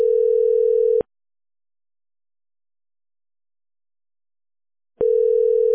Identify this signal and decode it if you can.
Ringback tone (North America), 440 + 480 Hz.